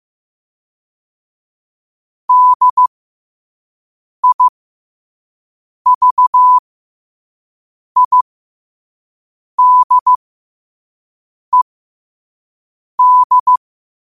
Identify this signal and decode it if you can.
Morse code, heard as a keyed tone, about 15 wpm, spaced as for 5 wpm, 1000 Hz. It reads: DIVIDED